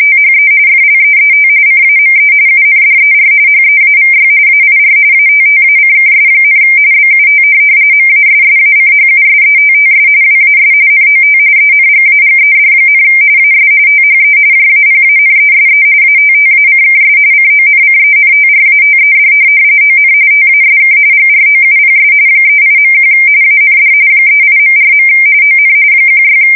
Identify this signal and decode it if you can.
RTTY, about 45.45 baud. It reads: THE FLEET CAME UNDER HEAVY AIR ATTACKS, BUT MASSACHUSETTS WAS NOT ENGAGED, AS THE JAPANESE ATTACKS CONCENTRATED ON TASK GROUPS 38.1 AND 38.4. ON 16
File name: fsk170 / USS-Massachusetts